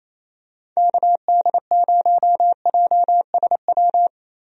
Morse code, heard as a keyed tone, about 28 wpm, 700 Hz. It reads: KD0JSW
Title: USA Call Signs - 6 Characters - Repeated 3x - 28wpm